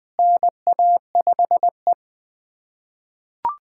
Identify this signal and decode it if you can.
Morse code, heard as a keyed tone, about 20 wpm, 700 Hz. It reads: NA5E
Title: USA Callsign DE Callsign - 4 Characters - 20wpm